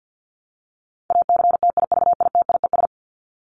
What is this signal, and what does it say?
Morse code, heard as a keyed tone, about 50 wpm, 700 Hz. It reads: ACTIVITIES
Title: ES:S Suffix - Top 100 Words with Base Words 50wpm